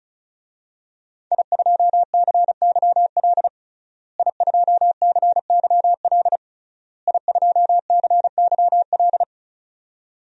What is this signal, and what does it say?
Morse code, heard as a keyed tone, about 36 wpm, 700 Hz. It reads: I2CYL I2CYL I2CYL
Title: DXCC-100 Call Signs - 5 Characters - Repeated 3x - 35wpm